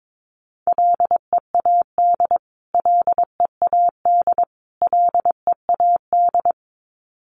Morse code, heard as a keyed tone, about 22 wpm, 700 Hz. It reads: LEAD LEAD LEAD
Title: Top 500 Words Repeated 22wpm